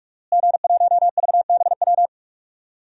Morse code, heard as a keyed tone, about 46 wpm, 700 Hz. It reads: G1VBW
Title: DXCC-100 Call Signs - 5 Characters - Repeated 2x - 45wpm